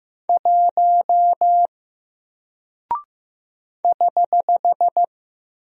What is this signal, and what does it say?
Morse code, heard as a keyed tone, about 15 wpm, 700 Hz. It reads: 1 <HH>
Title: Single Character (Letter:Number:Punctuation:Prosign) Alt 15wpm